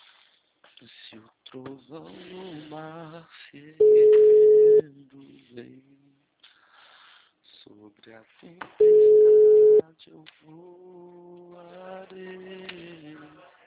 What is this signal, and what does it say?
Ringback tone (Europe, 425 Hz), with speech.